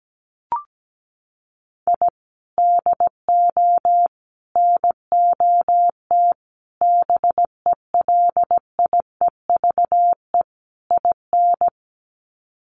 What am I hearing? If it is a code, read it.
Morse code, heard as a keyed tone, about 17 wpm, 700 Hz. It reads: I DO NOT BELIEVE IN
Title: Sets of 5 Words Encore 17wpm